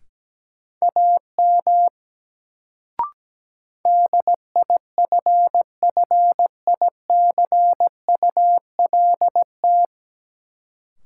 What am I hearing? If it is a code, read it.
Morse code, heard as a keyed tone, about 17 wpm, 700 Hz. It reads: AM DIFFICULT
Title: Sentences from Top 600 Words - New Word Drill - 17wpm